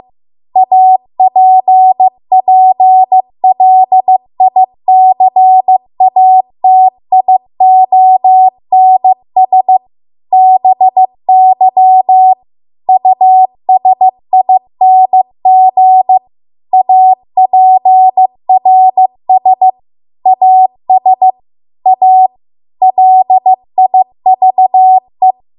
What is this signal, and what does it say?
Morse code, heard as a keyed tone, about 15 wpm, 750 Hz. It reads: APPLICATIONS BY USING APRS AS A LIVE